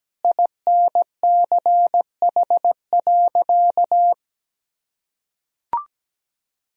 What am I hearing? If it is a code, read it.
Morse code, heard as a keyed tone, about 17 wpm, 700 Hz. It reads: INCH.
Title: Sentences from Top 1000 Words - Review Part 3 - 17wpm